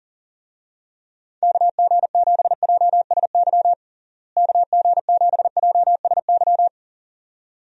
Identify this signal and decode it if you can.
Morse code, heard as a keyed tone, about 40 wpm, 700 Hz. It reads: KG7JSY KG7JSY
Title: USA Call Signs - 6 Characters - Repeated 2x - 40wpm